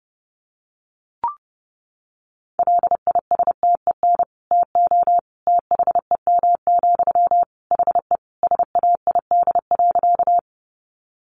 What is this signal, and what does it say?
Morse code, heard as a keyed tone, about 30 wpm, 700 Hz. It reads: LISTEN TO THEM, HE SAID.
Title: Sentences from Top 400 Words 30wpm